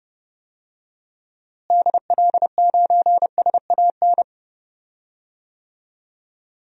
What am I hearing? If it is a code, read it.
Morse code, heard as a keyed tone, about 32 wpm, 700 Hz. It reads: DL9SAN
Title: DXCC-100 Call Signs - 6 Characters - Repeated 2x - 30wpm